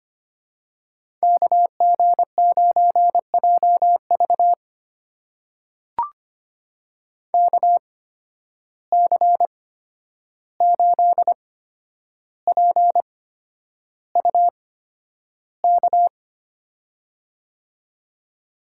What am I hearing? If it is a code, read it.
Morse code, heard as a keyed tone, about 25 wpm, spaced as for 10 wpm, 700 Hz. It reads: KG9JV KC8PUK
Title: USA Call Signs - 8x Character Spacing - 25wpm